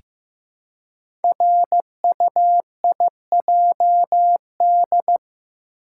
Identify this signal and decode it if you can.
Morse code, heard as a keyed tone, about 15 wpm, 700 Hz. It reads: RUIJD
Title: Code Groups - 5 Letters - 15wpm